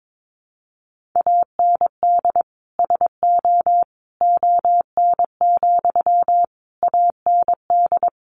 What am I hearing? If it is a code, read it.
Morse code, heard as a keyed tone, about 22 wpm, 700 Hz. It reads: AND SO ON, AND
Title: Sentences from Top 100 Words 22wpm